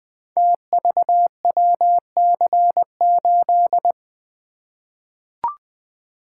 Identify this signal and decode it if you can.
Morse code, heard as a keyed tone, about 20 wpm, 700 Hz. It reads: TVWC8
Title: Code Groups - 5 Letter:Numbers - 20wpm